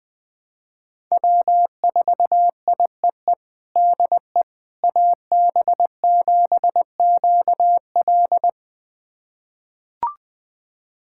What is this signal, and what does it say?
Morse code, heard as a keyed tone, about 20 wpm, 700 Hz. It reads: W4IEE DE AB7QL
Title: USA Callsign DE Callsign - 5 Characters - 20wpm